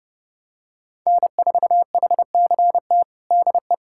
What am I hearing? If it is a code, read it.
Morse code, heard as a keyed tone, about 32 wpm, 700 Hz. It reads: N4HCT DE
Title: USA Callsign DE Callsign - 5 Characters - 30wpm